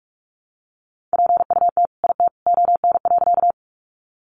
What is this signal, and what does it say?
Morse code, heard as a keyed tone, about 44 wpm, 700 Hz. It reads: PUT IT ON.
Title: Sentences from Top 200 Words 45wpm